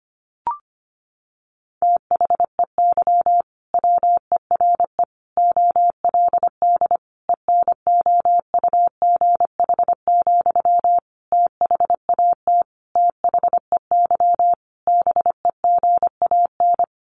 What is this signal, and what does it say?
Morse code, heard as a keyed tone, about 25 wpm, 700 Hz. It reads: THEY WERE OLD ENOUGH, THAT THEY BEGAN
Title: Sentences from Top 800 Words 25wpm